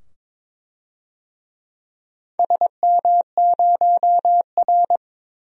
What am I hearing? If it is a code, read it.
Morse code, heard as a keyed tone, about 22 wpm, 700 Hz. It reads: SM0R
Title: DXCC-100 Call Signs - 4 Characters - 22wpm